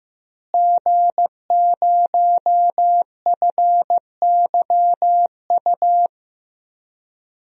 Morse code, heard as a keyed tone, about 15 wpm, 700 Hz. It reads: G0FYU
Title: DXCC-100 Call Signs - 5 Characters - Repeated 2x - 15wpm